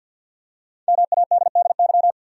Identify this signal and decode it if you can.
Morse code, heard as a keyed tone, about 51 wpm, 700 Hz. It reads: MADDY
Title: USA Common Names - Max 5 Letters - 50wpm